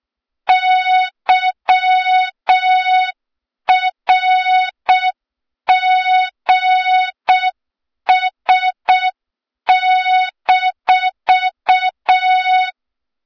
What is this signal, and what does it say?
Morse code, heard as a keyed tone, about 6 wpm, 3700 Hz, with noise. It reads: YRGS-
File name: Morze-HA2MN / hang / 23 GYAKORLAT